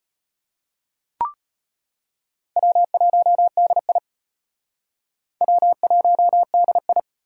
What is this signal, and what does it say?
Morse code, heard as a keyed tone, about 32 wpm, 700 Hz. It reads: W1DI W1DI